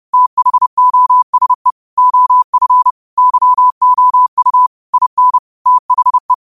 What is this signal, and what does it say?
Morse code, heard as a keyed tone, about 30 wpm, 1000 Hz. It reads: THOSE OF YOU IN THE